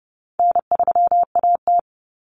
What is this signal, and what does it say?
Morse code, heard as a keyed tone, about 30 wpm, 700 Hz. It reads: N3AT
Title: Common Cut Numbers - 4-Digit 30wpm